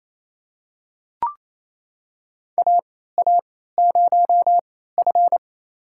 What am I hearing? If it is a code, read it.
Morse code, heard as a keyed tone, about 28 wpm, spaced as for 16 wpm, 700 Hz. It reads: AA0F
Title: USA Call Signs - 4 Characters - 3x Character Spacing - 28wpm